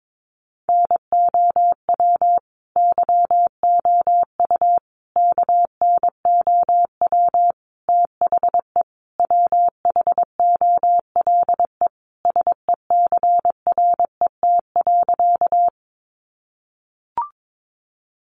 Morse code, heard as a keyed tone, about 22 wpm, 700 Hz. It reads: NOW YOU KNOW THE WHOLE SECRET.
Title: Sentences from Top 1200 Words 22wpm